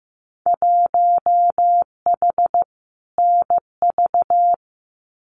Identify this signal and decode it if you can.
Morse code, heard as a keyed tone, about 15 wpm, 700 Hz. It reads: 1H NV
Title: Winter Field Day Exchanges 15wpm